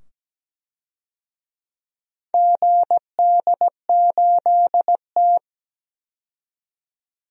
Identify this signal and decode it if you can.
Morse code, heard as a keyed tone, about 17 wpm, 700 Hz. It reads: GD8T
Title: DXCC-100 Call Signs - 4 Characters - 17wpm